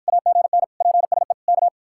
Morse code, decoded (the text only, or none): AGN PSE K